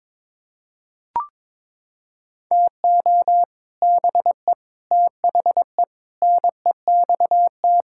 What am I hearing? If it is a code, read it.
Morse code, heard as a keyed tone, about 21 wpm, 700 Hz. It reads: TO BE THE NEXT
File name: Sets of 9 Words - 22wpm